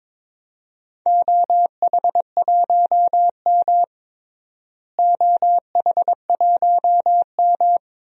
Morse code, heard as a keyed tone, about 22 wpm, 700 Hz. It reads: OH1M OH1M
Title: DXCC-100 Call Signs - 4 Characters - Repeated 3x - 22wpm